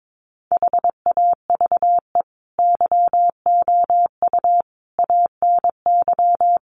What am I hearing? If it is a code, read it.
Morse code, heard as a keyed tone, about 22 wpm, 700 Hz. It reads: HAVE YOU ANY
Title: Sentences from Top 1400 Words 22wpm